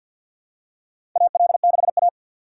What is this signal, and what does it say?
Morse code, heard as a keyed tone, about 52 wpm, 700 Hz. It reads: AL6A
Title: USA Call Signs - 4 Characters - 2x Character Spacing - 50wpm